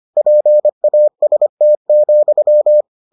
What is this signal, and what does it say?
Morse code, heard as a keyed tone, about 25 wpm, 600 Hz. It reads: PAST,